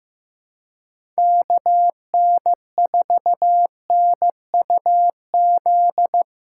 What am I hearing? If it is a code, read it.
Morse code, heard as a keyed tone, about 15 wpm, 700 Hz. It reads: KN4NUZ
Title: USA Callsign DE Callsign 15wpm